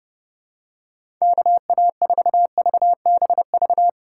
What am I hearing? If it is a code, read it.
Morse code, heard as a keyed tone, about 30 wpm, 700 Hz. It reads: KA4VBV
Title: USA Call Signs - 2x Character Spacing - 30wpm